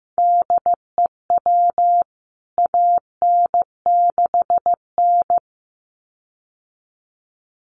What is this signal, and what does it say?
Morse code, heard as a keyed tone, about 15 wpm, 700 Hz. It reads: DEW AN6N
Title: CWT Exchange 15wpm